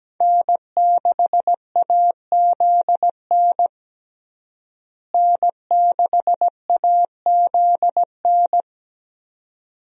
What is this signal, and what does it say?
Morse code, heard as a keyed tone, about 17 wpm, 700 Hz. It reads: N6AZN N6AZN
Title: USA Call Signs - 5 Characters - Repeated 2x - 17wpm